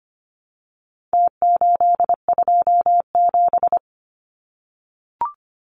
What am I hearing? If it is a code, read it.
Morse code, heard as a keyed tone, about 25 wpm, 700 Hz. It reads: T827
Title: Numbers - Contest Style - 4-Digit 25wpm